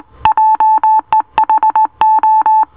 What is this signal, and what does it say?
Morse code, heard as a keyed tone, about 18 wpm, 870 Hz. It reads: JEHO